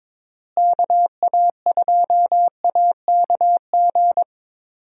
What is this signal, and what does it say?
Morse code, heard as a keyed tone, about 22 wpm, 700 Hz. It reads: KA2AKG
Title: USA Call Signs - 6 Characters - Repeated 3x - 22wpm